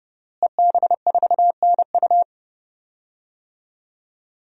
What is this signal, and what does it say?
Morse code, heard as a keyed tone, about 30 wpm, 700 Hz. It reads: EB4NU
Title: DXCC-100 Call Signs - 5 Characters - Repeated 2x - 30wpm